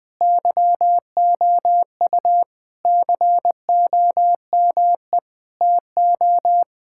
Morse code, heard as a keyed tone, about 20 wpm, 700 Hz. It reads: YOU COME TO
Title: Sentences from Top 1600 Words 20wpm